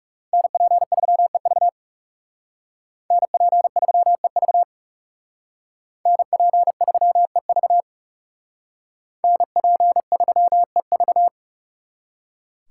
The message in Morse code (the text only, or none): NP3EV NP3EV NP3EV NP3EV